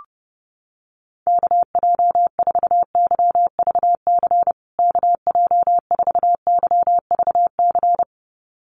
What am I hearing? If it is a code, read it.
Morse code, heard as a keyed tone, about 30 wpm, 700 Hz. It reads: KJ4YVC KJ4YVC
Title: USA Call Signs Repeated 2x 30wpm